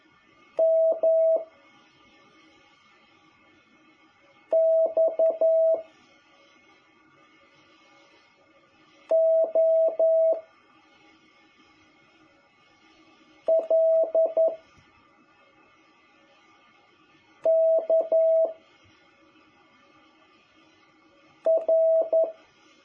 Morse code, heard as a keyed tone, about 11 wpm, 650 Hz. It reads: M X O L K R
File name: jh8cbh / 2021cw